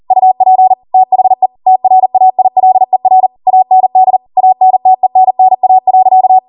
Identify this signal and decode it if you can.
Morse code, heard as a keyed tone, about 40 wpm, 750 Hz. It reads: UP THE TRAILER AND ANTENNA.